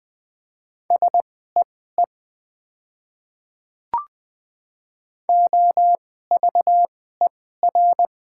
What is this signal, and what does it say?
Morse code, heard as a keyed tone, about 20 wpm, spaced as for 14 wpm, 700 Hz. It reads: SEE OVER
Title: 2x Character Spacing - Top 200 Words - 20wpm